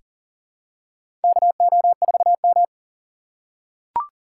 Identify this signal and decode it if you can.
Morse code, heard as a keyed tone, about 37 wpm, 700 Hz. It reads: KO4M
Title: USA Call Signs - 4 Characters - 6x Character Spacing - 40wpm